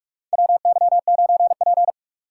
Morse code, heard as a keyed tone, about 44 wpm, 700 Hz. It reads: WY9P
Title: USA Call Signs - 6x Character Spacing - 45wpm